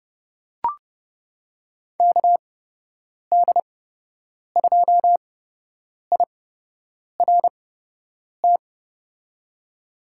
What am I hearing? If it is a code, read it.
Morse code, heard as a keyed tone, about 30 wpm, spaced as for 8 wpm, 700 Hz. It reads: KD2IRT
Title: USA Call Signs - 6 Characters - 8x Character Spacing - 30wpm